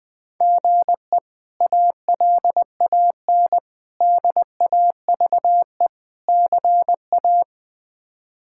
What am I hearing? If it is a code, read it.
Morse code, heard as a keyed tone, about 20 wpm, 700 Hz. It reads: GE ALAN DAVE CA
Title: K1USN SST QSO Practice - 20wpm